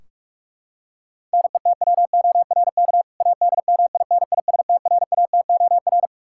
Morse code, heard as a keyed tone, about 41 wpm, 700 Hz. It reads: NETWORK ADMINISTRATOR
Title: K1USN SST QSO Practice 45wpm